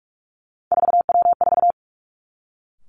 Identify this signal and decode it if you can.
Morse code, heard as a keyed tone, about 46 wpm, 700 Hz. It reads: 4W4
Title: Three Challenging Letter-Number Combinations 45wpm